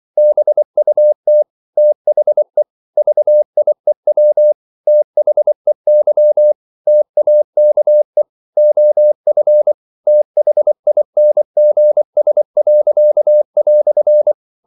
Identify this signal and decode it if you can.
Morse code, heard as a keyed tone, about 24 wpm, 600 Hz. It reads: BUT THE VIEW THEY TAKE OF THINGS."
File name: CW / Practice / EnglishQuotations0324